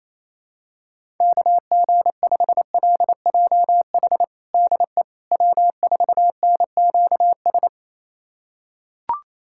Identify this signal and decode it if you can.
Morse code, heard as a keyed tone, about 28 wpm, 700 Hz. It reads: KG5LJH DE W4NQS
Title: USA Callsign DE Callsign 28wpm